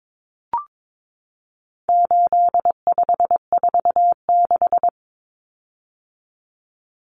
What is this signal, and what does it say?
Morse code, heard as a keyed tone, about 22 wpm, 700 Hz. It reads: 8546
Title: Common Cut Numbers - 4-Digit 22wpm